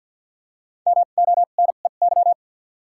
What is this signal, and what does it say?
Morse code, heard as a keyed tone, about 47 wpm, 700 Hz. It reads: M O N E Y
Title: 2x Character Spacing - Top 500 Words - 50wpm